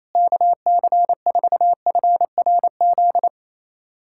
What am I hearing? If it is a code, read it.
Morse code, heard as a keyed tone, about 28 wpm, 700 Hz. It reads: KC4FRZ
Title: USA Callsign DE Callsign - 6 Characters - 28wpm